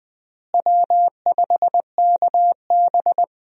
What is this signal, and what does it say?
Morse code, heard as a keyed tone, about 20 wpm, 700 Hz. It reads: W5KB
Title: USA Call Signs - 4 Characters - Repeated 2x - 20wpm